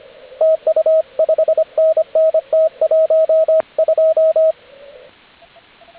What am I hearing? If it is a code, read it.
Morse code, heard as a keyed tone, about 25 wpm, 640 Hz, with radio noise. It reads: TU 5NN T12